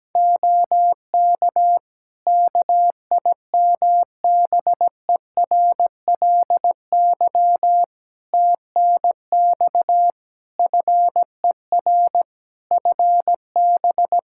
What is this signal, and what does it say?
Morse code, heard as a keyed tone, about 17 wpm, 700 Hz. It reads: OK KIMBERLY TNX FER FB
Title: Practice QSO - Exchange 4 - Ending - 17wpm